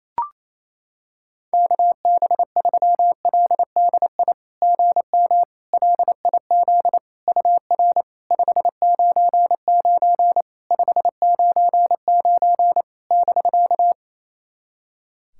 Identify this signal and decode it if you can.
Morse code, heard as a keyed tone, about 28 wpm, 700 Hz. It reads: KB3LDI GM LIZ UR 599 599 <BK>
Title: SOTA QSO Practice 28wpm